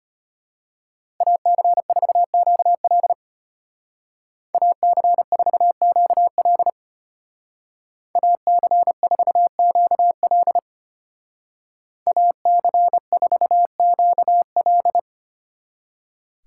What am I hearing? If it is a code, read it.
Morse code, heard as a keyed tone, about 34 wpm, 700 Hz. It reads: AC4QL AC4QL AC4QL AC4QL